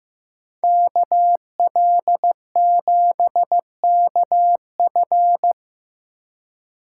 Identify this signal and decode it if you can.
Morse code, heard as a keyed tone, about 18 wpm, 700 Hz. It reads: KL7KF
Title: Speed Racing - USA 2x2 Call Signs 15wpm